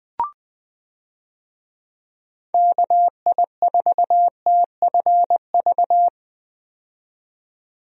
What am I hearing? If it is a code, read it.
Morse code, heard as a keyed tone, about 21 wpm, 700 Hz. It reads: KI4TFV
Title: USA Call Signs - 6 Characters - 20wpm